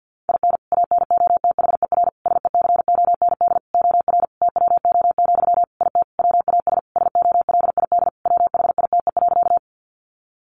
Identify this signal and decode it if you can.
Morse code, heard as a keyed tone, about 50 wpm, 700 Hz. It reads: IN ANOTHER SECOND OR TWO, IT WAS SOLID WHITE.
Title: Sentences from Top 1500 Words 50wpm